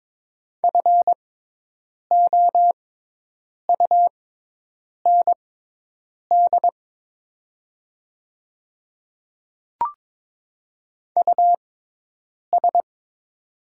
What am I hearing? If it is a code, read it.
Morse code, heard as a keyed tone, about 22 wpm, spaced as for 8 wpm, 700 Hz. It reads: FOUND US
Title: 6x Character Spacing - Top 300 Words - 22wpm